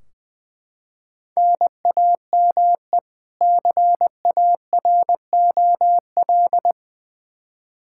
Morse code, heard as a keyed tone, about 20 wpm, 700 Hz. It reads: NAME CAROL
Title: Practice QSO - Exchange 1 - Beginning - 20wpm